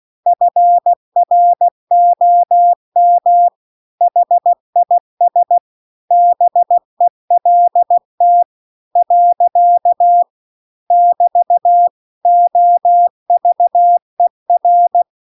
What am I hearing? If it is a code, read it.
Morse code, heard as a keyed tone, about 16 wpm, 700 Hz. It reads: FROM HIS BELT . <BT> OVER